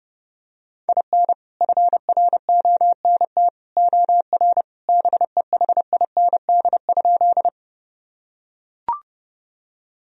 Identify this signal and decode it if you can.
Morse code, heard as a keyed tone, about 32 wpm, 700 Hz. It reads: IN FRONT OR BEHIND?
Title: Sentences from Top 1000 Words - Review Part 2 - 30wpm